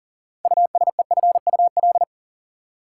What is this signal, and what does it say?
Morse code, heard as a keyed tone, about 40 wpm, 700 Hz. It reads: USEFUL
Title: Six Letter Words - 40wpm